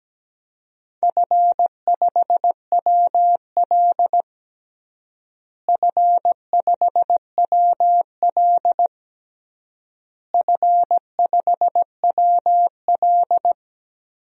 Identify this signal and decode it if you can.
Morse code, heard as a keyed tone, about 17 wpm, 700 Hz. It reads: F5WL F5WL F5WL
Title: DXCC-100 Call Signs - 4 Characters - Repeated 3x - 17wpm